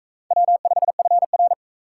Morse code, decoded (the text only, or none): W5FR